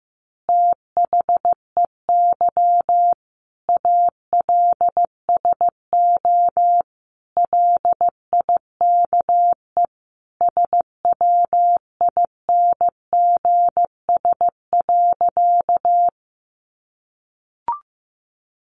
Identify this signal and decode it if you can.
Morse code, heard as a keyed tone, about 15 wpm, 700 Hz. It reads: THEY ALSO LIKE SWINGS.
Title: Easy Phrases 15wpm